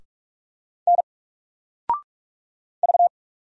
Morse code, heard as a keyed tone, about 44 wpm, 700 Hz. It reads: N V
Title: Single Letter-Number Alt 45wpm